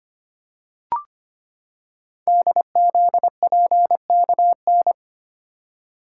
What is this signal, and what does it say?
Morse code, heard as a keyed tone, about 25 wpm, 700 Hz. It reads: DZPKN
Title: Code Groups - 5 Letters - 25wpm